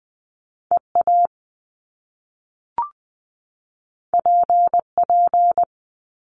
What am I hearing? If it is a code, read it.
Morse code, heard as a keyed tone, about 20 wpm, 700 Hz. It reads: EA PP
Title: Top Two Letter Combinations in Words 20wpm